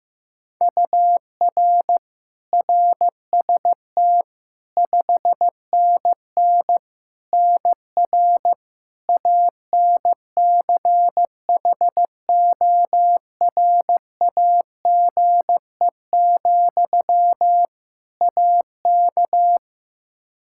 Morse code, heard as a keyed tone, about 15 wpm, 700 Hz. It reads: UR RST 5NN NR ANCHORAGE, AK
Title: Practice QSO - Exchange 1 - Beginning - 15wpm